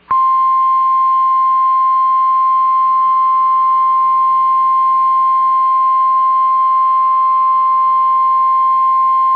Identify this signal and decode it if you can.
Emergency Alert System (SAME) tones (the 1050 Hz attention tone). It is heard from 0.5 to 8.9 s.